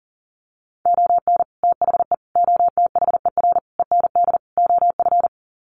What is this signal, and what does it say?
Morse code, heard as a keyed tone, about 40 wpm, 700 Hz. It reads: ON THE OTHER END OF
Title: Sets of 5 Words Encore 40wpm